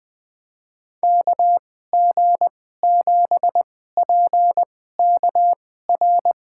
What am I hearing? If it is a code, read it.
Morse code, heard as a keyed tone, about 21 wpm, spaced as for 15 wpm, 700 Hz. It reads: KG7PKR
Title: USA Call Signs - 6 Characters - 2x Character Spacing - 20wpm